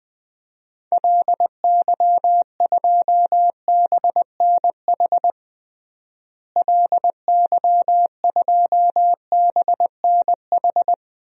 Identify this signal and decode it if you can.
Morse code, heard as a keyed tone, about 21 wpm, 700 Hz. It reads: LY2BNH LY2BNH
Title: DXCC-100 Call Signs - 6 Characters - Repeated 3x - 20wpm